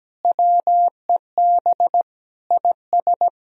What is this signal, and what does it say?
Morse code, heard as a keyed tone, about 17 wpm, 700 Hz. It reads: WEB IS
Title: Sentences from Top 1000 Words - Review Part 4 - 17wpm